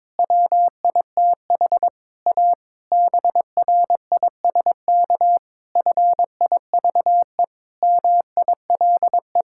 Morse code, heard as a keyed tone, about 22 wpm, 700 Hz. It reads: WITH A BRISK FIVE MILE
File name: Difficult Sentences with Hints - 22wpm